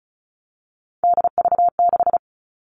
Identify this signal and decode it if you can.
Morse code, heard as a keyed tone, about 34 wpm, 700 Hz. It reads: DV6
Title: All Cut Numbers - 3-Digit 35wpm